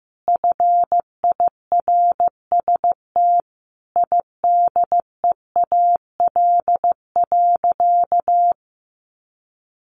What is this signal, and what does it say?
Morse code, heard as a keyed tone, about 15 wpm, 700 Hz. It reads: FIRST IDEAL.
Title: Sentences from Top 1400 Words 15wpm